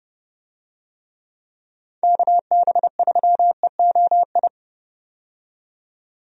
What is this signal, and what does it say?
Morse code, heard as a keyed tone, about 30 wpm, 700 Hz. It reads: KB3EOI
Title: USA Call Signs - 6 Characters - 30wpm